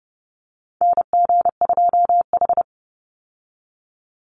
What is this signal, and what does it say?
Morse code, heard as a keyed tone, about 30 wpm, 700 Hz. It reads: NG2H
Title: Call Signs Repeated 3x - 30wpm